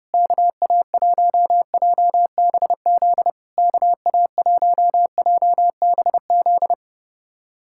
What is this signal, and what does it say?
Morse code, heard as a keyed tone, about 30 wpm, 700 Hz. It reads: KA1JBZ KA1JBZ
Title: SOTA QSO Practice 30wpm